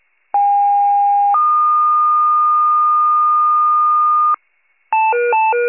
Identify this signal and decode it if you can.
Two-tone page, heard as a one-second tone, then a three-second tone, about 810 then 1220 Hz. It is heard from 0.3 to 4.3 s.